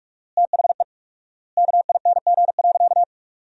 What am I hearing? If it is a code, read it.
Morse code, heard as a keyed tone, about 46 wpm, 700 Hz. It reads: THE KING.
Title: 4x Word-Spacing - Sentences from Top 400 Words - 45wpm